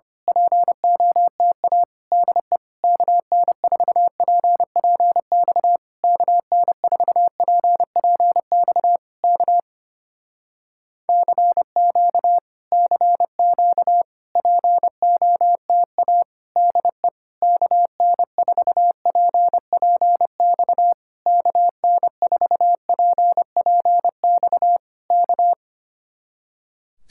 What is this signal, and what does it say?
Morse code, heard as a keyed tone, about 27 wpm, 700 Hz. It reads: POTA DE KN4PPX KN4PPX K CQ CQ POTA DE KN4PPX KN4PPX K
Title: Speed-Racing Common QSO Phrases - 25wpm